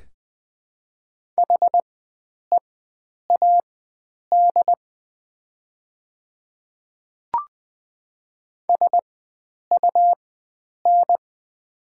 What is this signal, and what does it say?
Morse code, heard as a keyed tone, about 20 wpm, spaced as for 9 wpm, 700 Hz. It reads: HEAD SUN
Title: 4x Character Spacing - Top 500 Words - 20wpm